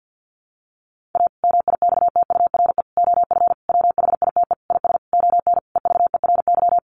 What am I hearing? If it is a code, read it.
Morse code, heard as a keyed tone, about 50 wpm, 700 Hz. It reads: A MIXTURE OF WHITE IS ON EVERY
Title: Sentences from Top 1500 Words 50wpm